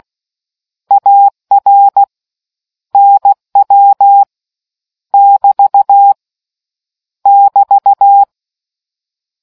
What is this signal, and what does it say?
Morse code, heard as a keyed tone, about 16 wpm, 800 Hz. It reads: AR NW <BT> <BT>